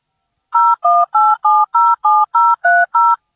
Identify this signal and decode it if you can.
DTMF, heard as telephone touch tones, tones of about 220 ms. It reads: 018707030